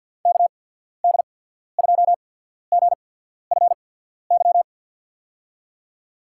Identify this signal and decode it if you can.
Morse code, heard as a keyed tone, about 52 wpm, 700 Hz. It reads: K D 2 G F Y
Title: USA Call Signs - 6 Characters - 8x Character Spacing - 50wpm